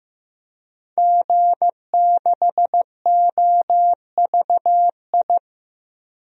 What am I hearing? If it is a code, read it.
Morse code, heard as a keyed tone, about 15 wpm, 700 Hz. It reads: G6OVI
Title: DXCC-100 Call Signs - 5 Characters - Repeated 2x - 15wpm